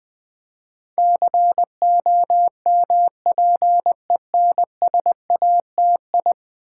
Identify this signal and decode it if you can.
Morse code, heard as a keyed tone, about 20 wpm, 700 Hz. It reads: COMPENSATI
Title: Build A Word - 5 Letters At A Time - 12-Letter Words - 20wpm